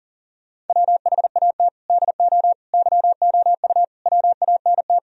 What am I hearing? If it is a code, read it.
Morse code, heard as a keyed tone, about 50 wpm, 700 Hz. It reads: WHAT DO YOU WANT
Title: Speed-Racing - Sets of 4 Words 40wpm